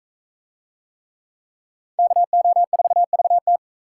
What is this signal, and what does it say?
Morse code, heard as a keyed tone, about 38 wpm, 700 Hz. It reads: KO4VT